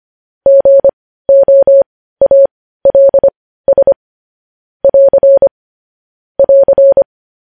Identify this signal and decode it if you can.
Morse code, heard as a keyed tone, about 25 wpm, spaced as for 15 wpm, 550 Hz. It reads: GOALS <AR> <AR>
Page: news-headlines-in-morse-code-at-15-wpm